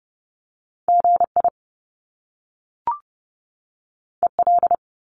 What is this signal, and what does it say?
Morse code, heard as a keyed tone, about 30 wpm, 700 Hz. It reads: GI EL
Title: Top Two Letter Combinations in Words 30wpm